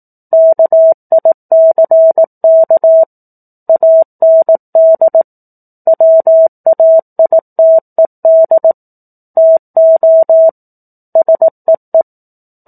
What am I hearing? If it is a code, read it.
Morse code, heard as a keyed tone, about 18 wpm, 650 Hz. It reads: KICK AND WAITED TO SEE
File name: 04-Chapter-04 18 WPM